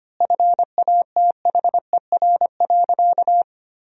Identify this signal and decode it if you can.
Morse code, heard as a keyed tone, about 25 wpm, 700 Hz. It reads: FATHER.
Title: Sentences from Top 1000 Words - Review Part 3 - 25wpm